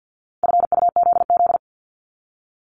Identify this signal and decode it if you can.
Morse code, heard as a keyed tone, about 50 wpm, 700 Hz. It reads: FUZZ
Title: Multi-letter Phonemes 50wpm